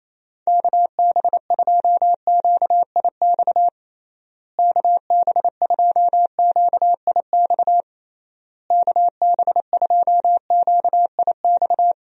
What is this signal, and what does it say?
Morse code, heard as a keyed tone, about 28 wpm, 700 Hz. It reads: KB2QIX KB2QIX KB2QIX
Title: USA Call Signs - 6 Characters - Repeated 3x - 28wpm